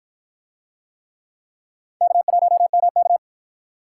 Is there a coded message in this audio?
Yes: Morse code from 2.0 to 3.2 s, heard as a keyed tone, about 53 wpm, 700 Hz. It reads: K1MK